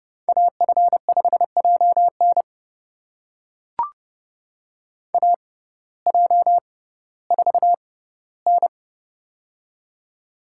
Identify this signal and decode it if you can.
Morse code, heard as a keyed tone, about 32 wpm, spaced as for 17 wpm, 700 Hz. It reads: AF5JN AJ4N
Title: USA Call Signs - 6x Character Spacing - 30wpm